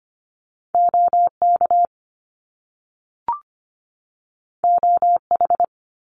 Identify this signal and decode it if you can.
Morse code, heard as a keyed tone, about 25 wpm, 700 Hz. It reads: OK OH
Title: Top Two Letter Combinations in Words 25wpm